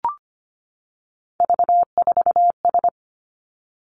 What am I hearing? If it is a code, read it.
Morse code, heard as a keyed tone, about 25 wpm, 700 Hz. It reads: V4S
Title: Three Challenging Letter-Number Combinations 25wpm